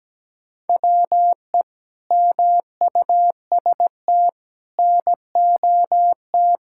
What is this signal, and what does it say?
Morse code, heard as a keyed tone, about 17 wpm, 700 Hz. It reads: WE MUST NOT
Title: Sentences from Top 1000 Words - Review Part 5 - 17wpm